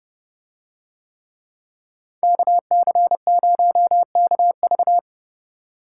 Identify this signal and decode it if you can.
Morse code, heard as a keyed tone, about 29 wpm, 700 Hz. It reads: KC0KV